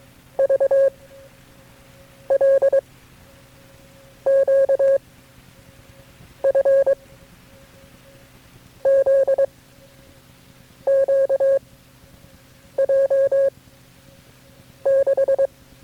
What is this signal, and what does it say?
Morse code, heard as a keyed tone, about 22 wpm, 540 Hz. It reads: V L Q F Z Q J 6
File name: FreeFiles / CW-practice / KB6MT-tape3-side1